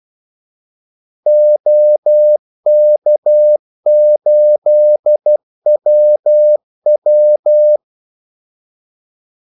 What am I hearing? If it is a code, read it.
Morse code, heard as a keyed tone, about 12 wpm, 600 Hz. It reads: OK8WW
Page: 2 ekstra ordmellemrum mellem kaldesignalerne